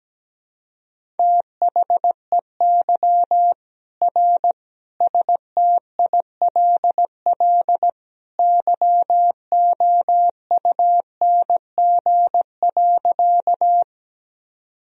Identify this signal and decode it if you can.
Morse code, heard as a keyed tone, about 17 wpm, 700 Hz. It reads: THEY R STILL YOUNG.
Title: Sentences from Top 400 Words and Abbreviations 17wpm